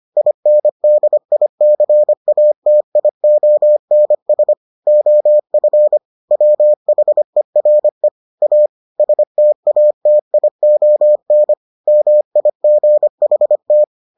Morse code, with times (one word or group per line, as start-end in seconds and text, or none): 0.2-4.5 INDICATIONS
4.9-6.0 OF
6.3-8.1 WHERE
8.4-8.7 A
9.0-11.5 STATION
11.9-13.8 MIGHT